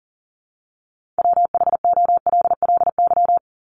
Morse code, heard as a keyed tone, about 39 wpm, 700 Hz. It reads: WHOLLY
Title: LY Suffix - Top 100 Words with Base Words 40wpm